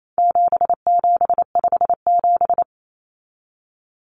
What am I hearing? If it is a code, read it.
Morse code, heard as a keyed tone, about 28 wpm, 700 Hz. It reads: 7757
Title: Numbers - 4-Digit 28wpm